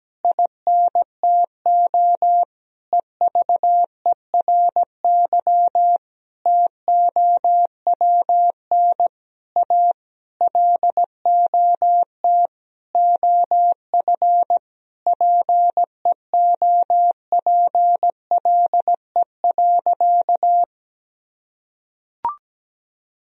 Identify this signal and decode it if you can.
Morse code, heard as a keyed tone, about 17 wpm, 700 Hz. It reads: INTO EVERY TOWN A LOT OF PEOPLE.
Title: Sentences from Top 1000 Words - Review Part 3 - 17wpm